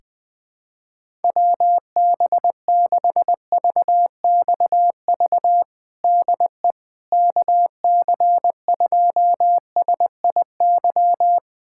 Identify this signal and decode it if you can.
Morse code, heard as a keyed tone, about 20 wpm, 700 Hz. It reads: WB6VXV DE KC2SIY
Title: USA Callsign DE Callsign 20wpm